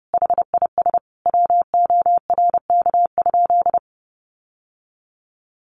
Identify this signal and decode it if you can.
Morse code, heard as a keyed tone, about 30 wpm, 700 Hz. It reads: HIS WORK?
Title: Sentences from Top 1300 Words 30wpm